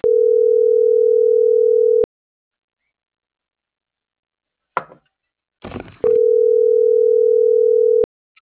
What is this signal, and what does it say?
Ringback tone (North America), 440 + 480 Hz.